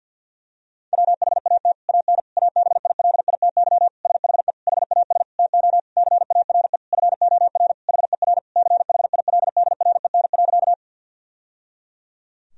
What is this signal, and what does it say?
Morse code, heard as a keyed tone, about 50 wpm, 700 Hz. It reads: WHAT AN ABILITY SHE HAS TO CARE FOR HER CHILDREN.
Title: Sentences from Top 1000 Words - Review Part 3 - 50wpm